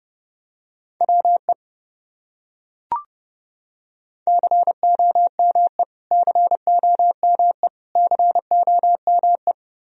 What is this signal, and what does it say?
Morse code, heard as a keyed tone, about 32 wpm, 700 Hz. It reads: WE COME COME COME
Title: Top 100 Words Repeated 30wpm